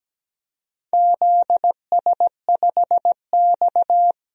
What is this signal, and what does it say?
Morse code, heard as a keyed tone, about 17 wpm, 700 Hz. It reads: ZS5X
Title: DXCC-100 Call Signs - 4 Characters - 17wpm